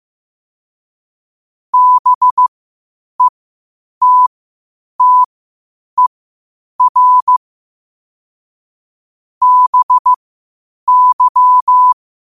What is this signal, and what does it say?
Morse code, heard as a keyed tone, about 15 wpm, spaced as for 8 wpm, 1000 Hz. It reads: BETTER BY
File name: qotdfiles / 8wpm